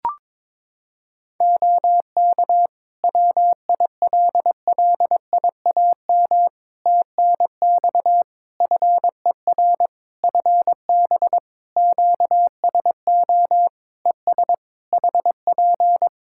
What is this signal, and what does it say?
Morse code, heard as a keyed tone, about 22 wpm, 700 Hz. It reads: OK WILLIAM TNX FER FB QSO ES HP
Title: Practice QSO - Exchange 4 - Ending - 22wpm